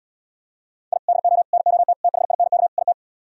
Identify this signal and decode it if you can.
Morse code, heard as a keyed tone, about 78 wpm, 700 Hz. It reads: I DO NOT THINK IT